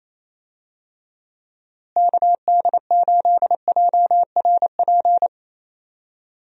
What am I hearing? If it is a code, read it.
Morse code, heard as a keyed tone, about 28 wpm, 700 Hz. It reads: KD8JRP
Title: USA Call Signs - 6 Characters - Repeated 3x - 28wpm